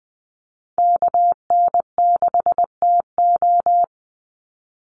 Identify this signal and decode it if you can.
Morse code, heard as a keyed tone, about 20 wpm, 700 Hz. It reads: KN6TO
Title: USA Call Signs 20wpm